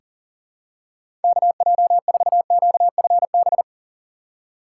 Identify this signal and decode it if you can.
Morse code, heard as a keyed tone, about 40 wpm, 700 Hz. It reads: KJ4QFB
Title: USA Call Signs - 6 Characters - 6x Character Spacing - 40wpm